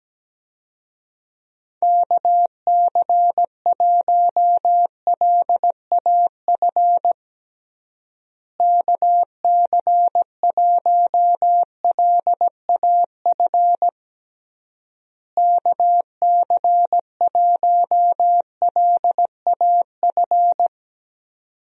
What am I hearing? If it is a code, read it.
Morse code, heard as a keyed tone, about 17 wpm, 700 Hz. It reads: KC1LAF KC1LAF KC1LAF
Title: USA Call Signs - 6 Characters - Repeated 3x - 17wpm